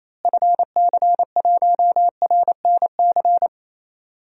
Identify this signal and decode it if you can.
Morse code, heard as a keyed tone, about 28 wpm, 700 Hz. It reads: FC1RNC
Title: DXCC-100 Call Signs - 6 Characters - Repeated 3x - 28wpm